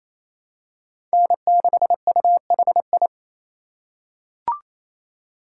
Morse code, heard as a keyed tone, about 28 wpm, 700 Hz. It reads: N6UHI
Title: USA Call Signs - 5 Characters - 2x Character Spacing - 28wpm